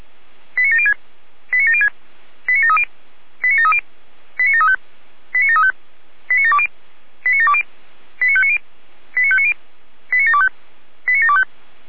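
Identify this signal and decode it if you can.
Five-tone selective call (ZVEI), heard as quick runs of five tones, tones of about 70 ms. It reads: 89706 89706 89730 89730 89735 89735 89720 89720 89590 89590 89725 89725
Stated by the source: ZVEI five-tone selective call received on an SDR.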